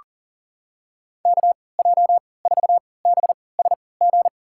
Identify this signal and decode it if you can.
Morse code, heard as a keyed tone, about 40 wpm, spaced as for 23 wpm, 700 Hz. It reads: KJ4BSG
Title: USA Call Signs - 6 Characters - 3x Character Spacing - 40wpm